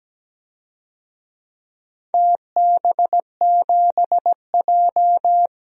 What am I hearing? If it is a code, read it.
Morse code, heard as a keyed tone, about 17 wpm, 700 Hz. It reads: TB7J
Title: DXCC-100 Call Signs - 4 Characters - Repeated 2x - 17wpm